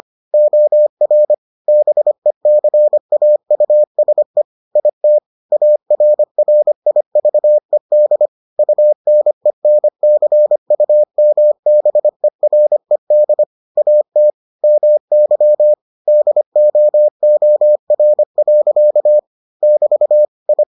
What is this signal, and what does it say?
Morse code, heard as a keyed tone, about 25 wpm, 600 Hz. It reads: OR BECAUSE IT ARRIVED UNENCUMBERED AT MY DOOR. <BT> I